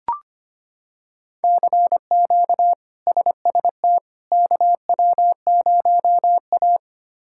Morse code, heard as a keyed tone, about 25 wpm, 700 Hz. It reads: CQ SST KW0A
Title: K1USN SST QSO Practice - 25wpm